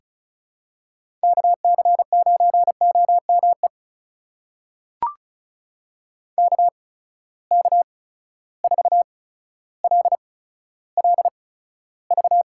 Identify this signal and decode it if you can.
Morse code, heard as a keyed tone, about 36 wpm, spaced as for 15 wpm, 700 Hz. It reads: KC9OME KK4LLV
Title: USA Call Signs - 6 Characters - 8x Character Spacing - 35wpm